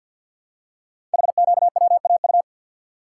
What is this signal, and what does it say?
Morse code, heard as a keyed tone, about 50 wpm, 700 Hz. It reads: HQWAU
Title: Code Groups - 5 Letters - 50wpm